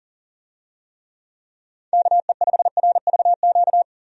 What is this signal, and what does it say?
Morse code, heard as a keyed tone, about 39 wpm, 700 Hz. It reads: KE5RVQ
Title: USA Call Signs - 6 Characters - 40wpm